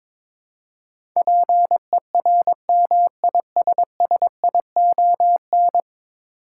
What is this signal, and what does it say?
Morse code, heard as a keyed tone, about 22 wpm, 700 Hz. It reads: PERMISSION
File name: practice / 22wpm / 10 Letter Words - First 3 Letters Given - 22wpm